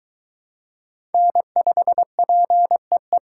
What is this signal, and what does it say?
Morse code, heard as a keyed tone, about 23 wpm, 700 Hz. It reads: N5PEE